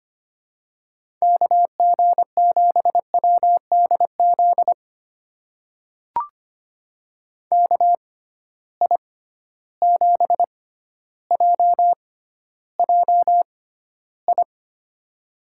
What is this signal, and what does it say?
Morse code, heard as a keyed tone, about 25 wpm, spaced as for 13 wpm, 700 Hz. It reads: KG7WDZ KI7JJI
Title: USA Call Signs - 6 Characters - 6x Character Spacing - 25wpm